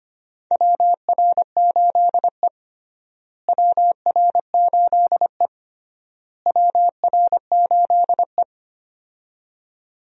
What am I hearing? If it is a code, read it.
Morse code, heard as a keyed tone, about 25 wpm, 700 Hz. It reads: WR8E WR8E WR8E
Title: USA Call Signs - 4 Characters - Repeated 3x - 25wpm